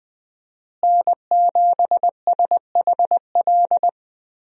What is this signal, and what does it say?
Morse code, heard as a keyed tone, about 20 wpm, 700 Hz. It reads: N7SHL
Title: USA Call Signs - 5 Characters - Repeated 2x - 20wpm